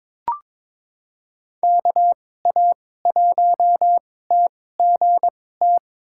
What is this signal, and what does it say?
Morse code, heard as a keyed tone, about 23 wpm, spaced as for 16 wpm, 700 Hz. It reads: KA1TGT
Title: USA Call Signs - 6 Characters - 2x Character Spacing - 22wpm